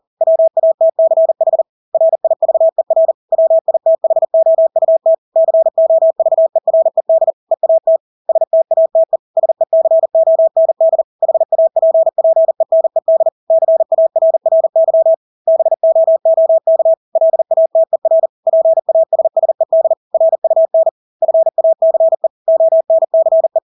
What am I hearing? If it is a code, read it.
Morse code, heard as a keyed tone, about 40 wpm, 650 Hz. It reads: WATCH RIVER WITHOUT COVERED EAT STATE SECOND HAPPENED CARRY BOOK LATER PASSED RUN FACE ONCE
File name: Engelske ord 40-40 wpm